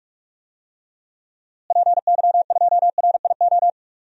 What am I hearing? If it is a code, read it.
Morse code, heard as a keyed tone, about 47 wpm, 700 Hz. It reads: PY2RIO
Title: DXCC-100 Call Signs - 6 Characters - Repeated 3x - 45wpm